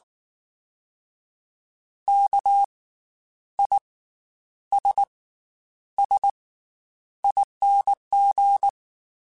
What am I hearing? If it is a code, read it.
Morse code, heard as a keyed tone, about 19 wpm, 800 Hz. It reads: K I S S ING
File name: Special INGs / Special INGs-mp3s / Slow ing @18